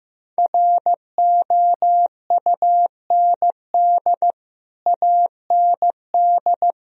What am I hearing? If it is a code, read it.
Morse code, heard as a keyed tone, about 15 wpm, 700 Hz. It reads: ROUND AND
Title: Sentences from Top 1000 Words - Review Part 3 - 15wpm